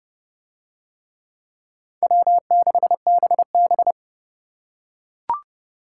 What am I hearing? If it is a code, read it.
Morse code, heard as a keyed tone, about 29 wpm, 700 Hz. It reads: W6BB